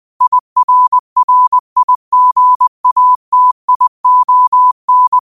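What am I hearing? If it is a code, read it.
Morse code, heard as a keyed tone, about 20 wpm, 1000 Hz. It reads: IRRIGATION
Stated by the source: Morse code practice at 20 wpm: famous quotations sent in telegraphy, from URVAG.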